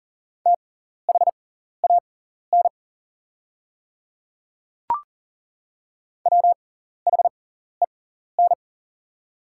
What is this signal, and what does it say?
Morse code, heard as a keyed tone, about 40 wpm, spaced as for 14 wpm, 700 Hz. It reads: THAN WHEN
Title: 6x Character Spacing - Top 100 Words - 40wpm